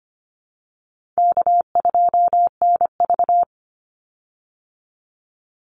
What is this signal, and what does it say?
Morse code, heard as a keyed tone, about 25 wpm, 700 Hz. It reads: K2NV
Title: Call Signs Repeated 3x - 25wpm